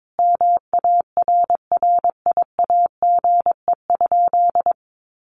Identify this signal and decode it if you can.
Morse code, heard as a keyed tone, about 22 wpm, 700 Hz. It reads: MARRIAGE?
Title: Sentences from Top 1400 Words 22wpm